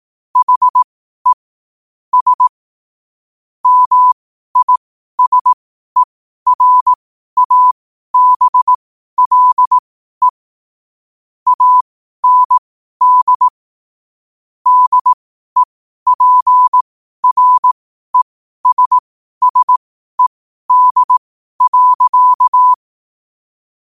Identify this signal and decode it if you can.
Morse code, heard as a keyed tone, about 18 wpm, spaced as for 12 wpm, 1000 Hz. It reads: HE S MISERABLE AND DEPRESSED.
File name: qotdfiles / 12wpm